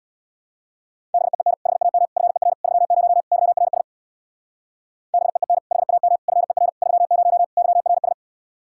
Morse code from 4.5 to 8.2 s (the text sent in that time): BEEN HAM FER 30 YRS